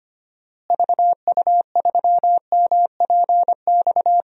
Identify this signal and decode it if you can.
Morse code, heard as a keyed tone, about 25 wpm, 700 Hz. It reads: VU3MPX
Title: DXCC-100 Call Signs - 6 Characters - Repeated 2x - 25wpm